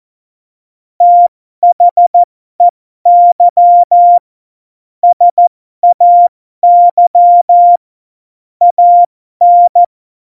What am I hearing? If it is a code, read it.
Morse code, heard as a keyed tone, about 14 wpm, spaced as for 12 wpm, 700 Hz. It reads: THEY SAY AN
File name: Morsekurs / Witze